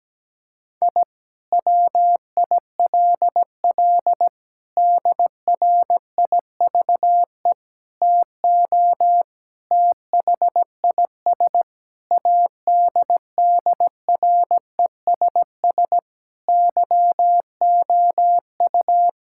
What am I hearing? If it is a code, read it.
Morse code, heard as a keyed tone, about 17 wpm, 700 Hz. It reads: I WILL DRIVE TO THIS ADDRESS YOU
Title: Sentences from Top 1000 Words - Review Part 6 - 17wpm